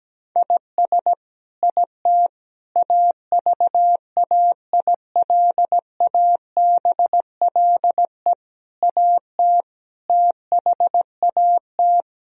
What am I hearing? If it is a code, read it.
Morse code, heard as a keyed tone, about 17 wpm, 700 Hz. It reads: IS IT AVAILABLE AT THAT
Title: Sentences from Top 1000 Words - Review Part 3 - 17wpm